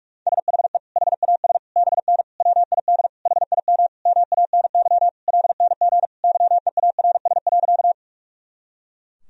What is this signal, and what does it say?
Morse code, heard as a keyed tone, about 46 wpm, 700 Hz. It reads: SHE HAS BN WID HIM MANY LNG YEARS.
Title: Sentences from Top 200 Words and Abbreviations 45wpm